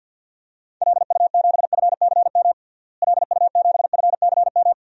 Morse code, heard as a keyed tone, about 50 wpm, 700 Hz. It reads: LU7FCM LU7FCM
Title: DXCC-100 Call Signs - 6 Characters - Repeated 2x - 50wpm